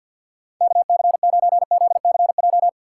Morse code, heard as a keyed tone, about 50 wpm, 700 Hz. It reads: KC9ZCJ
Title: USA Call Signs - 6 Characters - 3x Character Spacing - 50wpm